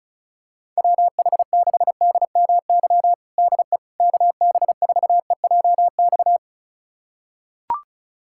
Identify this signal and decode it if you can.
Morse code, heard as a keyed tone, about 34 wpm, 700 Hz. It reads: WH6DMY DE KB4EJX
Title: USA Callsign DE Callsign - 6 Characters - 35wpm